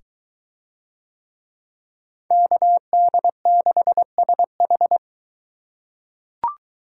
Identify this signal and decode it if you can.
Morse code, heard as a keyed tone, about 21 wpm, 700 Hz. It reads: KD6SH